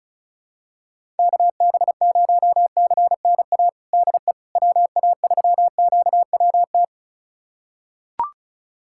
Morse code, heard as a keyed tone, about 36 wpm, 700 Hz. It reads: KB0CNA DE WA3QWT
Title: USA Callsign DE Callsign - 6 Characters - 35wpm